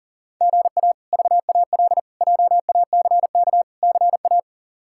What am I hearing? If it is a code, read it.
Morse code, heard as a keyed tone, about 40 wpm, 700 Hz. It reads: GA VAL JACK CA
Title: K1USN SST QSO Practice - 40wpm